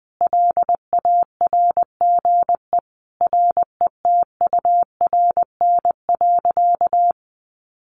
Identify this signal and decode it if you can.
Morse code, heard as a keyed tone, about 20 wpm, 700 Hz. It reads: LARGE RETURN.
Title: Sentences from Top 1100 Words 20wpm